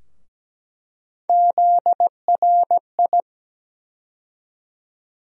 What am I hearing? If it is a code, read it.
Morse code, heard as a keyed tone, about 17 wpm, 700 Hz. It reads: ZRI
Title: USA Call Sign Postfixes - 3 Letters 17wpm